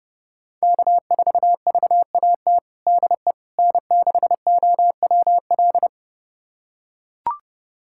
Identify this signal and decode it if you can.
Morse code, heard as a keyed tone, about 32 wpm, 700 Hz. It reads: K4VAT DE N6OWL
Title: USA Callsign DE Callsign - 5 Characters - 30wpm